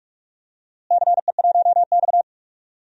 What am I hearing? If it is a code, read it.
Morse code, heard as a keyed tone, about 45 wpm, 700 Hz. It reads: CE1X
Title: DXCC-100 Call Signs - 4 Characters - Repeated 2x - 45wpm